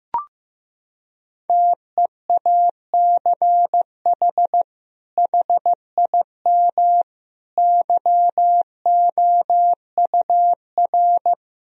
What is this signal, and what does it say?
Morse code, heard as a keyed tone, about 16 wpm, 700 Hz. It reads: TEACH HIM YOUR
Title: Sentences from Top 1000 Words - Review Part 3 - 15wpm